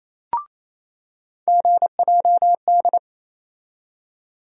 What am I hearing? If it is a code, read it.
Morse code, heard as a keyed tone, about 28 wpm, 700 Hz. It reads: GJD
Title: USA Call Sign Postfixes - 3 Letters 28wpm